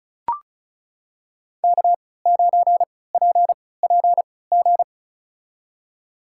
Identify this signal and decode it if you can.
Morse code, heard as a keyed tone, about 34 wpm, spaced as for 20 wpm, 700 Hz. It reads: K9PPG
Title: USA Call Signs - 3x Character Spacing - 35wpm